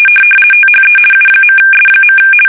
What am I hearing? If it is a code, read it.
RTTY, about 50 baud. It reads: THE U.S.
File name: fsk850-50B / teletype-corp